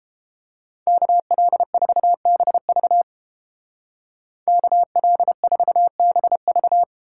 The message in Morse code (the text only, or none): KL4BV KL4BV